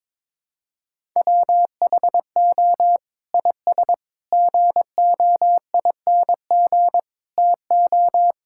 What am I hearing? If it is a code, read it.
Morse code, heard as a keyed tone, about 22 wpm, 700 Hz. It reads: WHO IS GOING TO
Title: Sentences from Top 1000 Words - Review Part 5 - 22wpm